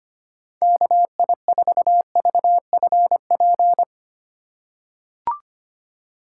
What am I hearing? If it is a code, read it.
Morse code, heard as a keyed tone, about 25 wpm, 700 Hz. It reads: KI4VFP
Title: USA Call Signs - 6 Characters - 3x Character Spacing - 25wpm